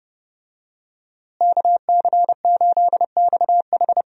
Morse code, heard as a keyed tone, about 30 wpm, 700 Hz. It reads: KC8XH
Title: USA Call Signs - 5 Characters - Repeated 3x - 30wpm